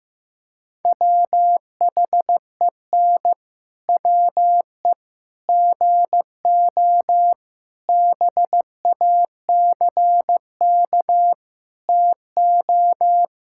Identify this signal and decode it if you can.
Morse code, heard as a keyed tone, about 15 wpm, 700 Hz. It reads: WHEN WE GO BACK TO
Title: Pangram Challenge v2 - 15wpm